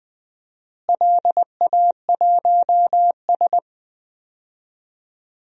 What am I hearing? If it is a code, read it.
Morse code, heard as a keyed tone, about 20 wpm, 700 Hz. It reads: LA1S
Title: DXCC-100 Call Signs - 4 Characters - Repeated 2x - 20wpm